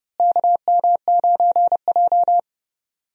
Morse code, heard as a keyed tone, about 32 wpm, 700 Hz. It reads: KM9J
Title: USA Call Signs - 4 Characters - Repeated 3x - 30wpm